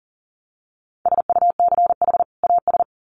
Morse code, heard as a keyed tone, about 40 wpm, 700 Hz. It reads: SUCH AS
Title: Sets of 2 Words 40wpm